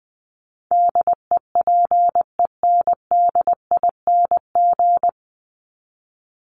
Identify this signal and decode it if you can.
Morse code, heard as a keyed tone, about 20 wpm, 700 Hz. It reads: DEPENDING
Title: ING Suffix - Top 100 Words with Base Words 20wpm